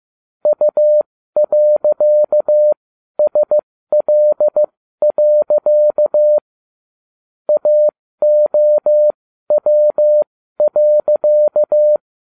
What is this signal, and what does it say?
Morse code, heard as a keyed tone, about 15 wpm, spaced as for 12 wpm, 600 Hz. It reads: U.SL. AOW.